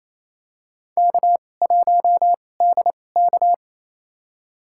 Morse code, heard as a keyed tone, about 28 wpm, spaced as for 20 wpm, 700 Hz. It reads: K1DK
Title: USA Call Signs - 4 Characters - 2x Character Spacing - 28wpm